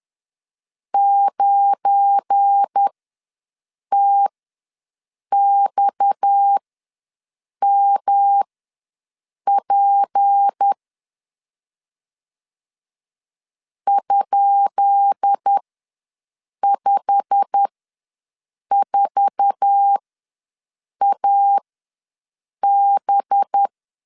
Morse code, heard as a keyed tone, about 10 wpm, spaced as for 6 wpm, 800 Hz. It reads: 9TXMP ?54AB
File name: LEZIONI / 10 SIGNS / 10B INT ws 10wpm 4134